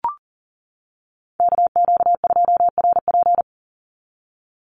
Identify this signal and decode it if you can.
Morse code, heard as a keyed tone, about 40 wpm, 700 Hz. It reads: KQ2RP
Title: NA SOTA Chaser Callsigns 40wpm